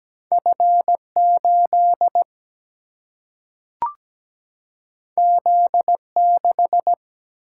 Morse code, heard as a keyed tone, about 17 wpm, 700 Hz. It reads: F8 Z6
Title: Single-Letter Single-Number 17wpm